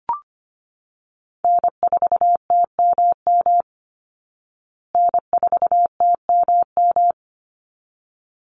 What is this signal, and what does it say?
Morse code, heard as a keyed tone, about 25 wpm, 700 Hz. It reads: N4TMM N4TMM
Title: Call Signs Repeated 3x - 25wpm